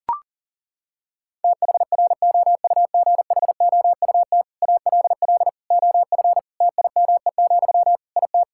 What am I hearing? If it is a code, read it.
Morse code, heard as a keyed tone, about 40 wpm, 700 Hz. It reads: THROUGHOUT ALL OF TIME, IT
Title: Sentences from Top 1000 Words - Review Part 5 - 40wpm